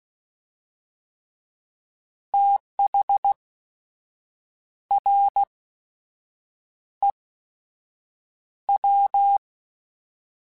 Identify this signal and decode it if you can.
Morse code, heard as a keyed tone, about 16 wpm, spaced as for 6 wpm, 800 Hz. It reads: THREW
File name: TH / TH mp3s / Fast TH @15